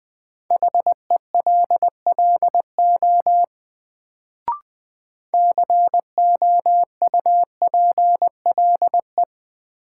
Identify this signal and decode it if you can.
Morse code, heard as a keyed tone, about 21 wpm, 700 Hz. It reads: HELLO COUPLE
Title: Sentences from Top 1100 Words - New Word Drill Alt - 20wpm